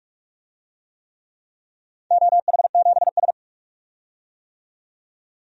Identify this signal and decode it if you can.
Morse code, heard as a keyed tone, about 45 wpm, 700 Hz. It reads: OH7S
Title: DXCC-100 Call Signs - 4 Characters - 45wpm